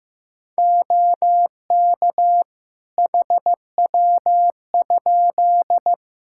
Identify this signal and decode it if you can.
Morse code, heard as a keyed tone, about 15 wpm, 700 Hz. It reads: OK HW?
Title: Practice QSO - Exchange 2 - Info - 15wpm